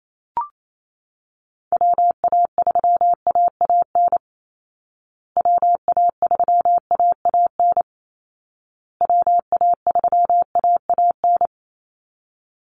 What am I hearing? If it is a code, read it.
Morse code, heard as a keyed tone, about 28 wpm, 700 Hz. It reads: WA3AAN WA3AAN WA3AAN
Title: Call Signs Repeated 2x - 28wpm